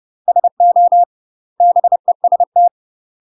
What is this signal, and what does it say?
Morse code, heard as a keyed tone, about 30 wpm, 700 Hz. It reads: SO BEST
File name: UK MASTER QSOs TRAINING / 30WPM / UK MASTER QSOs 0029